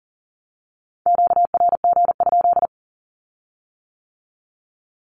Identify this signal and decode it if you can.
Morse code, heard as a keyed tone, about 41 wpm, 700 Hz. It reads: QRG?
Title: Q-Codes with Spoken Meaning - 40wpm